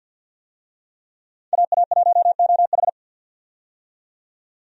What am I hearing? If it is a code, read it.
Morse code, heard as a keyed tone, about 49 wpm, 700 Hz. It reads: AA1OH
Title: USA Call Signs - 5 Characters - 50wpm